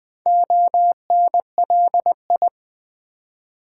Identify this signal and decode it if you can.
Morse code, heard as a keyed tone, about 20 wpm, 700 Hz. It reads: ONLI
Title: Build A Word - 2 Letters At A Time - 6-Letter Words - 20wpm